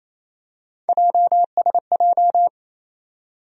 Morse code, heard as a keyed tone, about 28 wpm, 700 Hz. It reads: JSJ
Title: USA Call Sign Postfixes - 3 Letters 28wpm